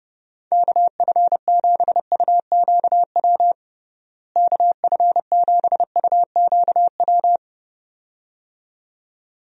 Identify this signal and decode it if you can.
Morse code, heard as a keyed tone, about 30 wpm, 700 Hz. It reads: KF7UQW KF7UQW
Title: USA Call Signs - 6 Characters - Repeated 2x - 30wpm